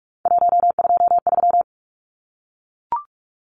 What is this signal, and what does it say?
Morse code, heard as a keyed tone, about 44 wpm, 700 Hz. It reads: 123
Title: Numbers - 3-Digit 45wpm